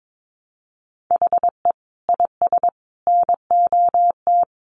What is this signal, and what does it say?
Morse code, heard as a keyed tone, about 22 wpm, 700 Hz. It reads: HE IS NOT
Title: Sentences from Top 700 Words 22wpm